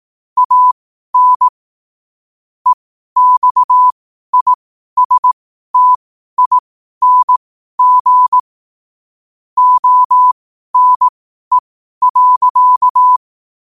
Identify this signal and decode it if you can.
Morse code, heard as a keyed tone, about 18 wpm, spaced as for 12 wpm, 1000 Hz. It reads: AN EXISTING ONE.
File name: qotdfiles / 12wpm